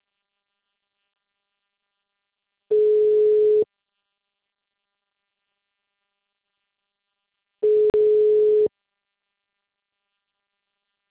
Ringback tone (Europe, 425 Hz).
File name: out- -2021- - - .27424